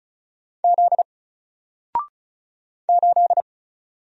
Morse code, heard as a keyed tone, about 35 wpm, 700 Hz. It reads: Z 8
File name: Single-Letter-Number-35wpm